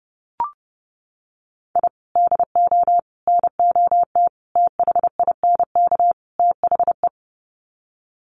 Morse code, heard as a keyed tone, about 28 wpm, 700 Hz. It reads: I DO NOT THINK THE
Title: Sets of 5 Words 30wpm